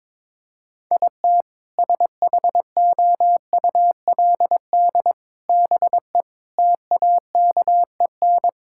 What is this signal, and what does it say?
Morse code, heard as a keyed tone, about 22 wpm, 700 Hz. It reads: IT SHOULD BE TAKEN
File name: practice / 22wpm / Sentences with 8-Word Phrases - 22wpm